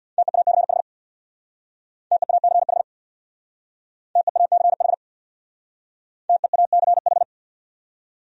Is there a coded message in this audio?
Yes: Morse code from 0.2 to 7.2 s, heard as a keyed tone, about 64 wpm, 700 Hz. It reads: TEACH TEACH TEACH TEACH